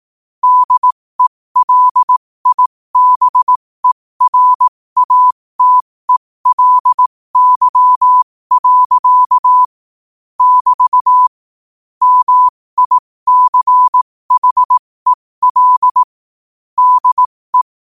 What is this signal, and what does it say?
Morse code, heard as a keyed tone, about 18 wpm, spaced as for 15 wpm, 1000 Hz. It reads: DELIBERATELY. <BT> MICHEL DE
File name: qotdfiles / 15wpm